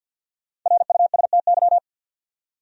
Morse code, heard as a keyed tone, about 49 wpm, 700 Hz. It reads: RUSTY
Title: USA Common Names - 50wpm